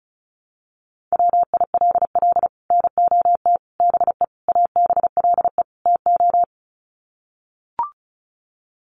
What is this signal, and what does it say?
Morse code, heard as a keyed tone, about 34 wpm, 700 Hz. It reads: WILL NOT BE ABLE TO
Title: Sets of 5 Words 35wpm